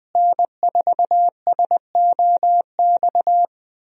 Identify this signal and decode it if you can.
Morse code, heard as a keyed tone, about 20 wpm, 700 Hz. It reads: N4SOX
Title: USA Call Signs - 5 Characters - Repeated 3x - 20wpm